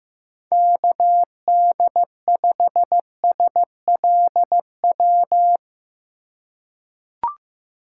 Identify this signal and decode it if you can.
Morse code, heard as a keyed tone, about 15 wpm, 700 Hz. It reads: KD5SLW
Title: USA Call Signs - 6 Characters - 2x Character Spacing - 15wpm